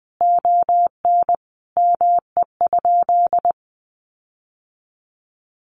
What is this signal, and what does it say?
Morse code, heard as a keyed tone, about 20 wpm, 700 Hz. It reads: ON ME?
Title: Sentences from Top 1500 Words 20wpm